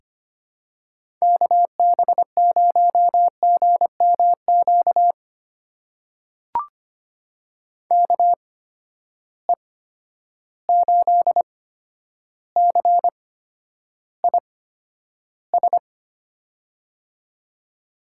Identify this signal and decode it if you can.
Morse code, heard as a keyed tone, about 25 wpm, spaced as for 11 wpm, 700 Hz. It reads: KB0GMQ KE8CIS
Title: USA Call Signs - 6 Characters - 8x Character Spacing - 25wpm